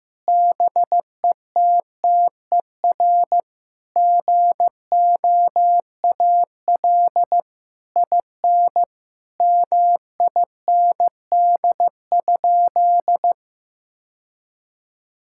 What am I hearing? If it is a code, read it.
Morse code, heard as a keyed tone, about 15 wpm, 700 Hz. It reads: BETTER GOAL IN MIND?
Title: Sentences from Top 1000 Words - Review Part 5 - 15wpm